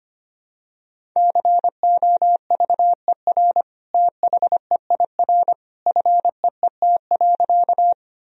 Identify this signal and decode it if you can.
Morse code, heard as a keyed tone, about 25 wpm, 700 Hz. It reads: COVER THEIR FEET.
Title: Sentences from Top 1000 Words - Review Part 1 - 25wpm